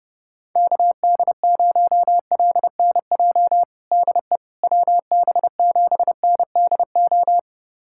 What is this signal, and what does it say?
Morse code, heard as a keyed tone, about 30 wpm, 700 Hz. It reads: KD0LNJ DE WB7NDO
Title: USA Callsign DE Callsign 30wpm